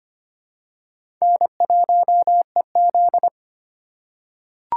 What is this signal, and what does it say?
Morse code, heard as a keyed tone, about 25 wpm, 700 Hz. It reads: N1EZ
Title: USA Call Signs - 4 Characters - 2x Character Spacing - 25wpm